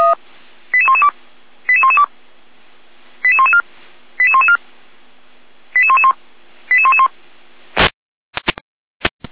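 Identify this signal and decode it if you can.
Five-tone selective call (ZVEI), heard as quick runs of five tones, tones of about 70 ms. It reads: 88112 88112 88114 88114 88111 88111
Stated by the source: ZVEI five-tone selective call received by SDR.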